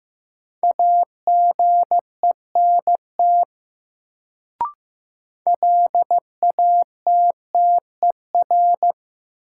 Morse code, heard as a keyed tone, about 15 wpm, 700 Hz. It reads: AGENT LATTER
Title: Sentences from Top 1600 Words - New Word Drill - 15wpm